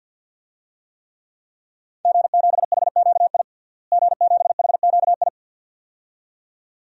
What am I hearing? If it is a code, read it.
Morse code, heard as a keyed tone, about 50 wpm, 700 Hz. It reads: G7HQI G7HQI
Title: DXCC-100 Call Signs - 5 Characters - Repeated 2x - 50wpm